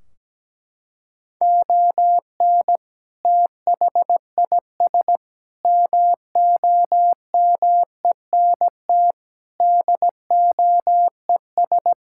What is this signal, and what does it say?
Morse code, heard as a keyed tone, about 17 wpm, 700 Hz. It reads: ON THIS MOMENT DOES
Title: Sentences from Top 1600 Words 17wpm